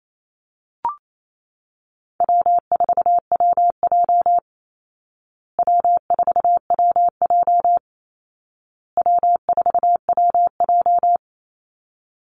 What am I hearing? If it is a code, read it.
Morse code, heard as a keyed tone, about 28 wpm, 700 Hz. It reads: W4WJ W4WJ W4WJ
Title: Easy Call Signs Repeated 3x - 28wpm